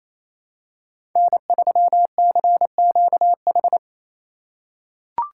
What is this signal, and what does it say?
Morse code, heard as a keyed tone, about 28 wpm, 700 Hz. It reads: N3CQH
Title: USA Call Signs - 3x Character Spacing - 28wpm